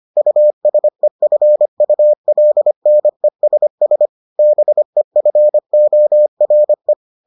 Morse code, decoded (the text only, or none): USEFULNESS BEFORE